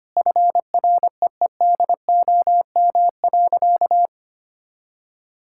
Morse code, heard as a keyed tone, about 25 wpm, 700 Hz. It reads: FREEDOM.
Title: Sentences from Top 1000 Words - Review Part 5 - 25wpm